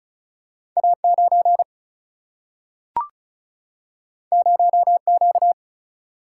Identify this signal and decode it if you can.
Morse code, heard as a keyed tone, about 36 wpm, 700 Hz. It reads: A9 0Q
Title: Single-Letter Single-Number 35wpm